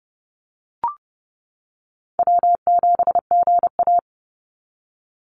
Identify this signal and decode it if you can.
Morse code, heard as a keyed tone, about 32 wpm, 700 Hz. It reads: W7GA
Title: NA SOTA Chaser Callsigns 30wpm